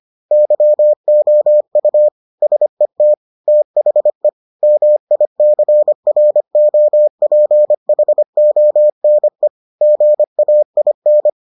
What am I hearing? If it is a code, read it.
Morse code, heard as a keyed tone, about 25 wpm, 600 Hz. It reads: YOU SET THE MICROPHONE GAIN